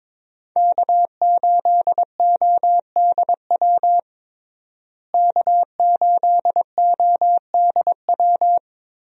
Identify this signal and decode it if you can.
Morse code, heard as a keyed tone, about 22 wpm, 700 Hz. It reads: K8ODW K8ODW
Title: USA Call Signs - 5 Characters - Repeated 3x - 22wpm